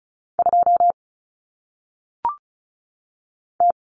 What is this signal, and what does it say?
Morse code, heard as a keyed tone, about 35 wpm, 700 Hz. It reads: 2 T
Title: Common Cut Numbers - 1-Digit 35wpm